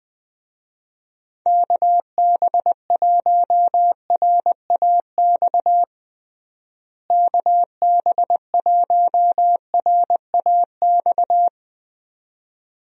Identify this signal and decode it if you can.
Morse code, heard as a keyed tone, about 21 wpm, 700 Hz. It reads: KB1RAX KB1RAX
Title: USA Call Signs - 6 Characters - Repeated 2x - 20wpm